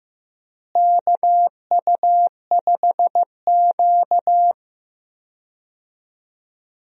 Morse code, heard as a keyed tone, about 15 wpm, 700 Hz. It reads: KU5Q
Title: USA Call Signs - 4 Characters - Repeated 3x - 15wpm